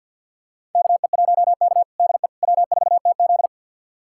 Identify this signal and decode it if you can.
Morse code, heard as a keyed tone, about 52 wpm, 700 Hz. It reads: KE1K DE W4TZ
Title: USA Callsign DE Callsign - 4 Characters - 50wpm